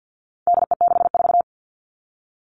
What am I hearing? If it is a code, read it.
Morse code, heard as a keyed tone, about 50 wpm, 700 Hz. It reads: DE64
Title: All Cut Numbers - 4-Digit 50wpm